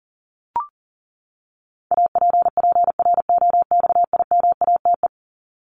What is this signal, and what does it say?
Morse code, heard as a keyed tone, about 40 wpm, 700 Hz. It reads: APPROXIMATE
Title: LY Suffix - Top 100 Words with Base Words 40wpm